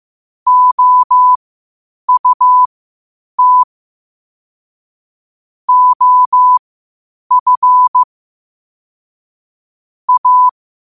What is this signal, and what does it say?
Morse code, heard as a keyed tone, about 15 wpm, spaced as for 8 wpm, 1000 Hz. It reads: OUT OF A
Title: 8 WPM morse code quotes for Thu, 29 Feb 2024